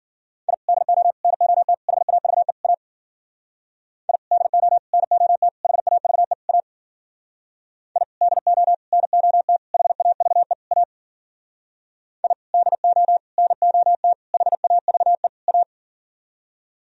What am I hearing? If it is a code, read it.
Morse code, heard as a keyed tone, about 55 wpm, 700 Hz. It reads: I DO NOT HAVE A I DO NOT HAVE A I DO NOT HAVE A I DO NOT HAVE A